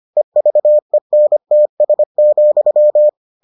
Morse code, heard as a keyed tone, about 25 wpm, 600 Hz. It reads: EVENTS,